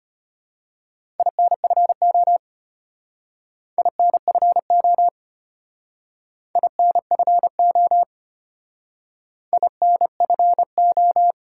Morse code, heard as a keyed tone, about 33 wpm, 700 Hz. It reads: INFO INFO INFO INFO